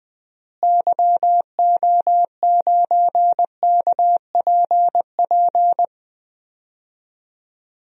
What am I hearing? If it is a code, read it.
Morse code, heard as a keyed tone, about 21 wpm, 700 Hz. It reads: YO9KPP
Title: DXCC-100 Call Signs - 6 Characters - Repeated 2x - 20wpm